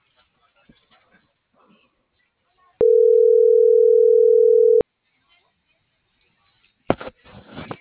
Ringback tone (North America), 440 + 480 Hz.